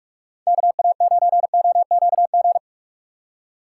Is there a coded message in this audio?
Yes: Morse code from 0.5 to 2.6 s, heard as a keyed tone, about 57 wpm, 700 Hz. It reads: KA9OQG